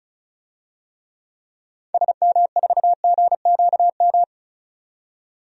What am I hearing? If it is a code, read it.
Morse code, heard as a keyed tone, about 36 wpm, 700 Hz. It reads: SM4GQM
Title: DXCC-100 Call Signs - 6 Characters - 35wpm